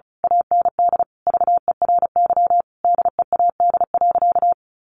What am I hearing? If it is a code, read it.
Morse code, heard as a keyed tone, about 34 wpm, 700 Hz. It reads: AND VERY DEAD.
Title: A Sound Of Thunder 35wpm